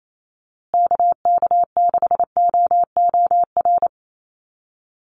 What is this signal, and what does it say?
Morse code, heard as a keyed tone, about 28 wpm, 700 Hz. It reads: KK6OOR
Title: USA Call Signs Repeated 3x 28wpm